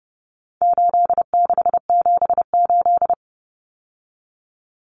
Morse code, heard as a keyed tone, about 30 wpm, 700 Hz. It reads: 8678
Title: Numbers - 4-Digit 30wpm